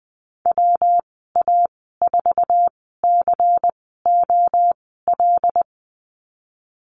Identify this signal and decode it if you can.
Morse code, heard as a keyed tone, about 20 wpm, spaced as for 15 wpm, 700 Hz. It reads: WA4COL
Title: 122 Lesson 20g - Number 3 - Call Signs 20wpm